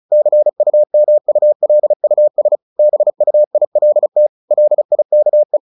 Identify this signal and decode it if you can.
Morse code, heard as a keyed tone, about 34 wpm, 600 Hz. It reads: CUMULUS BUILT LIKE
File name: the old man and the sea / the old man and the sea 1 35wpm